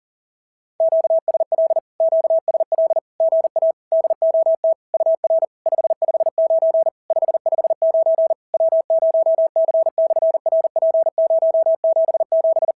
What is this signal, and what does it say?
Morse code, heard as a keyed tone, about 40 wpm, 650 Hz. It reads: QSL QSL GA DOT UR 559 559 W0C/RP077
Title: SOTA QSO Practice 40wpm